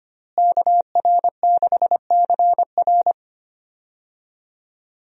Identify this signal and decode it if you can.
Morse code, heard as a keyed tone, about 25 wpm, 700 Hz. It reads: KR6CR
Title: USA Call Signs - 5 Characters - Repeated 3x - 25wpm